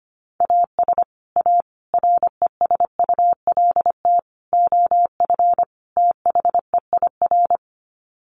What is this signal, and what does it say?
Morse code, heard as a keyed tone, about 25 wpm, 700 Hz. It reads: AS A RESULT OF THEIR
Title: Sets of 5 Words Encore2 25wpm